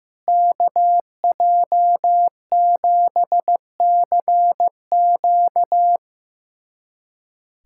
Morse code, heard as a keyed tone, about 15 wpm, 700 Hz. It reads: KJ7CQ
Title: USA Callsign DE Callsign - 5 Characters - 15wpm